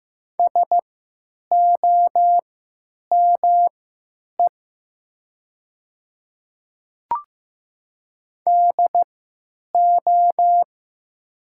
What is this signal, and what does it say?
Morse code, heard as a keyed tone, about 15 wpm, spaced as for 9 wpm, 700 Hz. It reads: SOME DO
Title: 3x Character Spacing - Top 100 Words - 15wpm